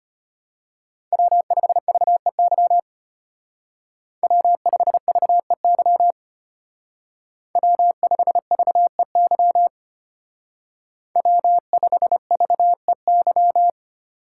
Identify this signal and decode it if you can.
Morse code, heard as a keyed tone, about 32 wpm, 700 Hz. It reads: W5VEY W5VEY W5VEY W5VEY